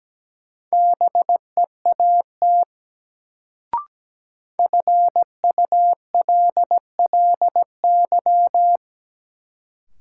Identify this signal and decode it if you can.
Morse code, heard as a keyed tone, about 17 wpm, 700 Hz. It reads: BEAT FULLY
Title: Sentences from Top 1300 Words - New Word Drill Alt - 17wpm